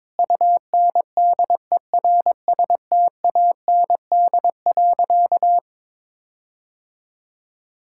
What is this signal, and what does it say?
Morse code, heard as a keyed tone, about 22 wpm, 700 Hz. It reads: UNDERSTAND.
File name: Sentences with 9-Word Phrases - 22wpm